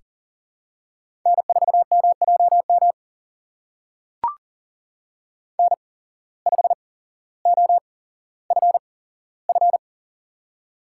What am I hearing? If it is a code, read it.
Morse code, heard as a keyed tone, about 37 wpm, spaced as for 17 wpm, 700 Hz. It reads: N4MJM N5OFF
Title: USA Call Signs - 8x Character Spacing - 40wpm